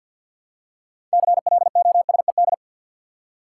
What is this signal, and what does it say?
Morse code, heard as a keyed tone, about 53 wpm, 700 Hz. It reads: CLOSED
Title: Six Letter Words - 50wpm